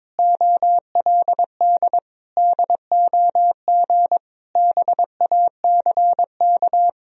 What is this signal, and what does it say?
Morse code, heard as a keyed tone, about 21 wpm, 700 Hz. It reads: OLD DOG BACK
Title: Sentences from Top 1000 Words - Review Part 2 - 22wpm